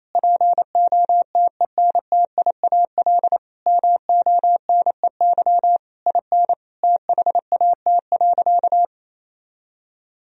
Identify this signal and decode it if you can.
Morse code, heard as a keyed tone, about 28 wpm, 700 Hz. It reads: POTENTIAL MONEY IN THAT.
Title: Sentences from Top 1000 Words - Review Part 4 - 28wpm